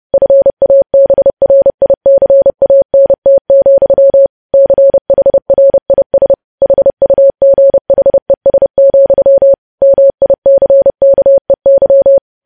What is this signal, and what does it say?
Morse code, heard as a keyed tone, about 30 wpm, 550 Hz. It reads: FABRICANT, CHRIS HUGHES, MICKEY
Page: news headlines in morse code at 30 wpm